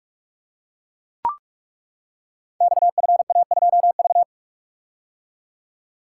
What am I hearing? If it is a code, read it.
Morse code, heard as a keyed tone, about 41 wpm, 700 Hz. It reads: XFA2V
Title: Code Groups - 5 Letter:Numbers - 45wpm